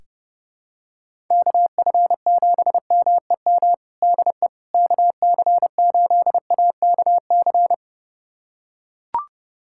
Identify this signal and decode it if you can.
Morse code, heard as a keyed tone, about 30 wpm, 700 Hz. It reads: KF7MEM DE KC8AKC
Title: USA Callsign DE Callsign - 6 Characters - 30wpm